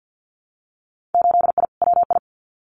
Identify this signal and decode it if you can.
Morse code, heard as a keyed tone, about 50 wpm, 700 Hz. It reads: 8I WI
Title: Winter Field Day Exchanges 50wpm